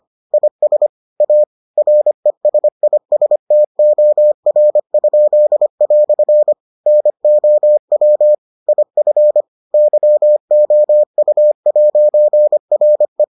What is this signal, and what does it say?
Morse code, heard as a keyed tone, about 25 wpm, 600 Hz. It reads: IS A RESISTOR?" NOW IF YOU'RE